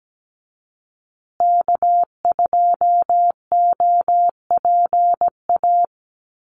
Morse code, heard as a keyed tone, about 17 wpm, 700 Hz. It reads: K2OPA
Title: USA Call Signs Repeated 3x 17wpm